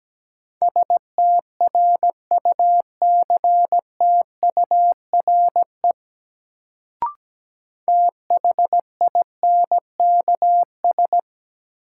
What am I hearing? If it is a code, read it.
Morse code, heard as a keyed tone, about 17 wpm, 700 Hz. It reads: STRUCTURE THINKS
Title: Sentences from Top 1000 Words - Review Part 6 - New Word Drill - 17wpm